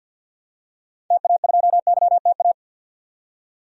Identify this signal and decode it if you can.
Morse code, heard as a keyed tone, about 46 wpm, 700 Hz. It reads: TA2YTA
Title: DXCC-100 Call Signs - 6 Characters - 50wpm